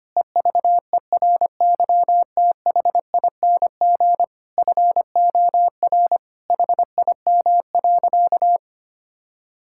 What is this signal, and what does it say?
Morse code, heard as a keyed tone, about 25 wpm, 700 Hz. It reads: EVERYTHING FOR HIM.
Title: Sentences from Top 1000 Words - Review Part 6 - 25wpm